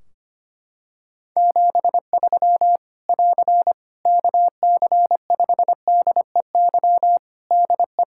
Morse code, heard as a keyed tone, about 25 wpm, 700 Hz. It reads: 73 <AR> KC5DEY DE
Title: Practice QSO - Exchange 4 - Ending - 25wpm